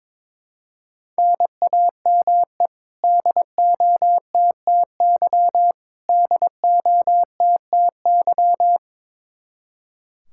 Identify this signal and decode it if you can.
Morse code, heard as a keyed tone, about 22 wpm, 700 Hz. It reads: NAME DOTTY DOTTY
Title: Practice QSO - Exchange 1 - Beginning - 22wpm